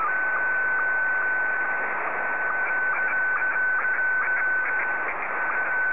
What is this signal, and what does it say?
SSTV (slow-scan television), Scottie 1.
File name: sstv / S1-VK7ZCF